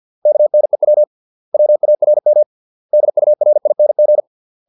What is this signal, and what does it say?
Morse code, heard as a keyed tone, about 50 wpm, 600 Hz. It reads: KNEW WARM DURING